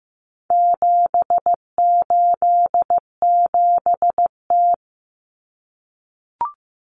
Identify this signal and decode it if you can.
Morse code, heard as a keyed tone, about 15 wpm, 700 Hz. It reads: 787T
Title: Common Cut Numbers - 4-Digit 15wpm